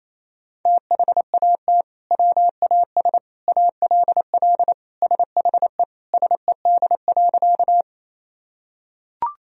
Morse code, heard as a keyed tone, about 28 wpm, 700 Hz. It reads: THAT WAS ALL SHE SED.
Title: Sentences from Top 100 Words and Abbreviations 28wpm